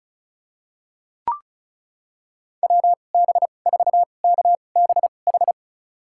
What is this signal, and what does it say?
Morse code, heard as a keyed tone, about 34 wpm, spaced as for 25 wpm, 700 Hz. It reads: WB4KBH
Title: USA Call Signs - 2x Character Spacing - 35wpm